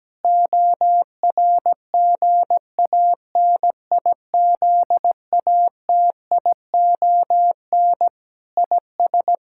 Morse code, heard as a keyed tone, about 17 wpm, 700 Hz. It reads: ORGANIZATION IS
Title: Sentences from Top 1000 Words - Review Part 5 - 17wpm